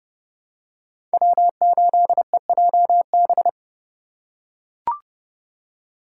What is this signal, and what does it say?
Morse code, heard as a keyed tone, about 28 wpm, 700 Hz. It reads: W8EJB